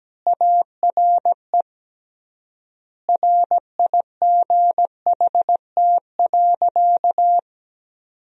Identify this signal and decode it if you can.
Morse code, heard as a keyed tone, about 17 wpm, 700 Hz. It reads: ARE RIGHT.
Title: 3x Word-Spacing - Sentences from Top 200 Words - 17wpm